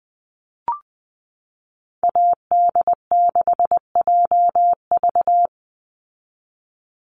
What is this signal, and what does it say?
Morse code, heard as a keyed tone, about 20 wpm, 700 Hz. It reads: AD6JV
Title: Call Signs Repeated 3x - 20wpm